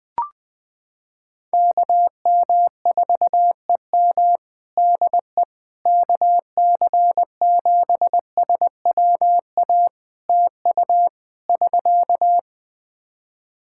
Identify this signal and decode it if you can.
Morse code, heard as a keyed tone, about 20 wpm, 700 Hz. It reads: KM4EM DE KC7SWA TU <SK>
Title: Practice QSO - Exchange 4 - Ending - 20wpm